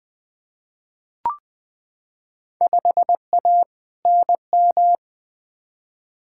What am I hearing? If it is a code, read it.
Morse code, heard as a keyed tone, about 20 wpm, 700 Hz. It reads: 5A NM
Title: Field Day Exchanges - 20wpm